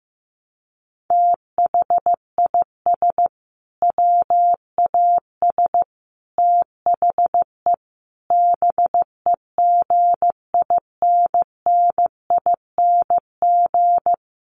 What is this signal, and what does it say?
Morse code, heard as a keyed tone, about 15 wpm, 700 Hz. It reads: THIS WAS THE BEGINNING
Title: Sentences from Top 900 Words 15wpm